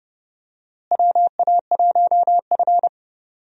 Morse code, heard as a keyed tone, about 30 wpm, 700 Hz. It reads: WA1F
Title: USA Call Signs - 4 Characters - Repeated 2x - 30wpm